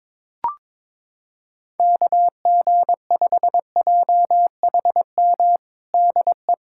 Morse code, heard as a keyed tone, about 21 wpm, 700 Hz. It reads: KG5JHM DE
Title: USA Callsign DE Callsign - 6 Characters - 22wpm